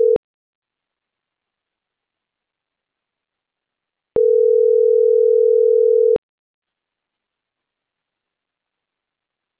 Ringback tone (North America), 440 + 480 Hz.